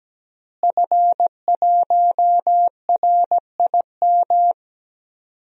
Morse code, heard as a keyed tone, about 17 wpm, 700 Hz. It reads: F1RIM
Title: DXCC-100 Call Signs - 5 Characters - Repeated 2x - 17wpm